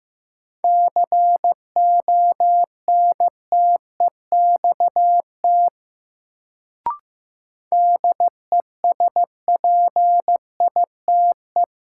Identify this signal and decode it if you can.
Morse code, heard as a keyed tone, about 15 wpm, 700 Hz. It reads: CONTEXT DESPITE
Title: Sentences from Top 1000 Words - Review Part 7 - New Word Drill - 15wpm